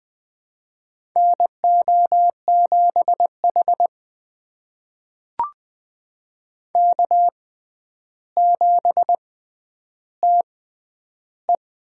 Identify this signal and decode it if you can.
Morse code, heard as a keyed tone, about 20 wpm, spaced as for 10 wpm, 700 Hz. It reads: NO7H K7TE
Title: USA Call Signs - 6x Character Spacing - 20wpm